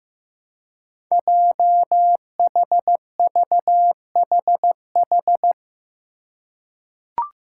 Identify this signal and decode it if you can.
Morse code, heard as a keyed tone, about 15 wpm, 700 Hz. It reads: JHVHH
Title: Five Challenging Letter-Number Combinations 15wpm